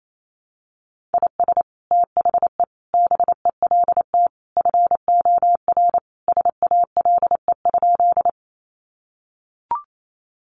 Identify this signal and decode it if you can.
Morse code, heard as a keyed tone, about 28 wpm, 700 Hz. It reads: IS THE BELT FOR SALE?
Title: Sentences from Top 1400 Words 28wpm